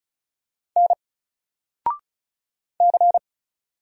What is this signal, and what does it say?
Morse code, heard as a keyed tone, about 36 wpm, 700 Hz. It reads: N C
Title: Single Character (Letter:Number:Punctuation:Prosign) 35wpm